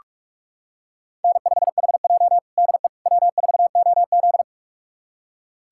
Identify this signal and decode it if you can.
Morse code, heard as a keyed tone, about 41 wpm, 700 Hz. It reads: N5HJ DE W4OZ
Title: USA Callsign DE Callsign - 4 Characters - 45wpm